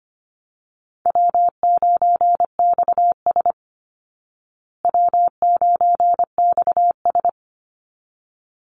Morse code, heard as a keyed tone, about 25 wpm, 700 Hz. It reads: W9XS W9XS
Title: Easy Call Signs Repeated 2x - 25wpm